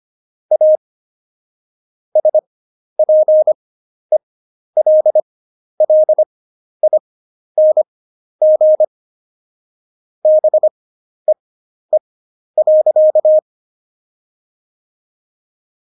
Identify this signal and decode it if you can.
Morse code, heard as a keyed tone, about 25 wpm, spaced as for 11 wpm, 620 Hz. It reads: A SPELLING BEE.
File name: Jokes-25-char-10-eff